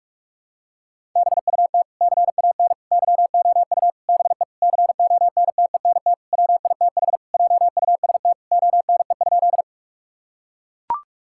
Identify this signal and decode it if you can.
Morse code, heard as a keyed tone, about 41 wpm, 700 Hz. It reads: BUT CAN YOU BE CONTENT WITH JUST ONE?
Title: Sentences from Top 1000 Words - Review Part 6 - 45wpm